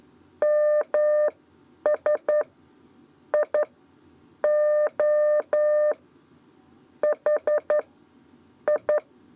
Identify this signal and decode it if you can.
Morse code, heard as a keyed tone, about 11 wpm, spaced as for 9 wpm, 600 Hz. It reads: MS I O H I
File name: Lezione trasmessa 2 lez 2 exe